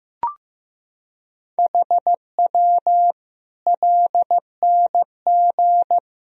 Morse code, heard as a keyed tone, about 15 wpm, 700 Hz. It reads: HW LNG
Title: Sentences from Top 500 Words and Abbreviations 15wpm